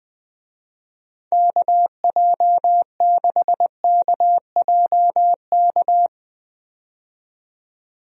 Morse code, heard as a keyed tone, about 20 wpm, 700 Hz. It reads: KJ6KJK
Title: USA Call Signs - 6 Characters - 20wpm